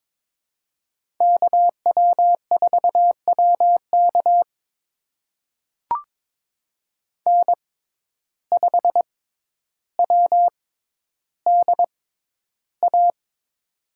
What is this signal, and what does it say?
Morse code, heard as a keyed tone, about 22 wpm, spaced as for 11 wpm, 700 Hz. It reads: KW4WK N5WDA
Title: USA Call Signs - 6x Character Spacing - 22wpm